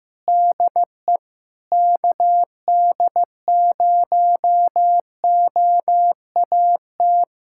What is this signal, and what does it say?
Morse code, heard as a keyed tone, about 15 wpm, 700 Hz. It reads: DE KD0OAT
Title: USA Callsign DE Callsign 15wpm